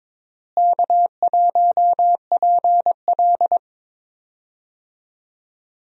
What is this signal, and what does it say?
Morse code, heard as a keyed tone, about 22 wpm, 700 Hz. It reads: K1PL
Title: USA Call Signs - 4 Characters - Repeated 2x - 22wpm